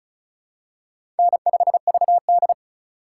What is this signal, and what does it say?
Morse code, heard as a keyed tone, about 34 wpm, 700 Hz. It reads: N5VD
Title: USA Call Signs - 4x Character Spacing - 35wpm